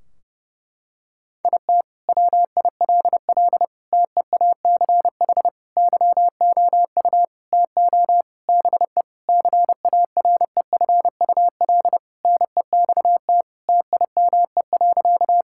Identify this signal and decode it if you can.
Morse code, heard as a keyed tone, about 30 wpm, 700 Hz. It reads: IT WILL TEACH YOU TO BE CAREFUL NEXT TIME.
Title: Sentences from Top 1600 Words 30wpm